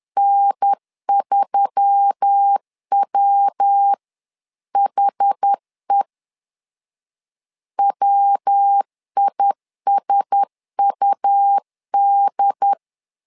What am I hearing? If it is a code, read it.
Morse code, heard as a keyed tone, about 10 wpm, 800 Hz. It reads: N3WHE WISUD
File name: LEZIONI / 4 Q1S3 10wpm 2427